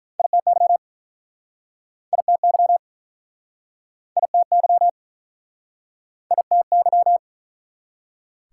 Morse code, heard as a keyed tone, about 49 wpm, 700 Hz. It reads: ITY ITY ITY ITY